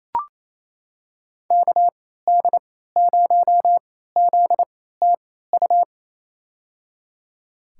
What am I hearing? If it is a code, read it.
Morse code, heard as a keyed tone, about 27 wpm, spaced as for 16 wpm, 700 Hz. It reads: KD0ZTU
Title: USA Call Signs - 3x Character Spacing - 28wpm